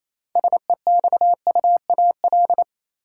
Morse code, heard as a keyed tone, about 27 wpm, 700 Hz. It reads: SEXUAL
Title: Six Letter Words - 28wpm